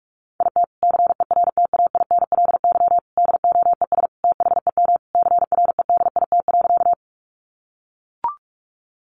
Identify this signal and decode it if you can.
Morse code, heard as a keyed tone, about 44 wpm, 700 Hz. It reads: IT CERTAINLY DOES THEM CREDIT.
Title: Sentences from Top 1000 Words 45wpm